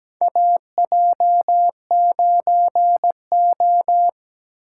Morse code, heard as a keyed tone, about 17 wpm, 700 Hz. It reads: AJ9O
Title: USA Call Signs - 2x Character Spacing - 17wpm